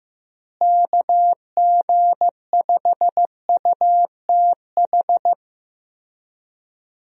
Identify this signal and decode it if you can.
Morse code, heard as a keyed tone, about 15 wpm, 700 Hz. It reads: KG5UTH
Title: USA Call Signs - 6 Characters - Repeated 3x - 15wpm